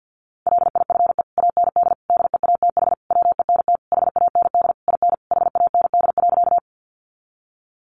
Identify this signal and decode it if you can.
Morse code, heard as a keyed tone, about 50 wpm, 700 Hz. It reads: LIFE AND DEATH WENT HAND IN HAND.
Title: Sentences from Top 1100 Words 50wpm